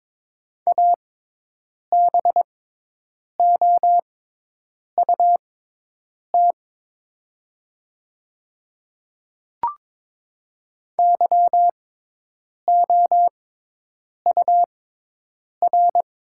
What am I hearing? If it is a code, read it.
Morse code, heard as a keyed tone, about 23 wpm, spaced as for 8 wpm, 700 Hz. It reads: ABOUT YOUR
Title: 6x Character Spacing - Top 100 Words - 22wpm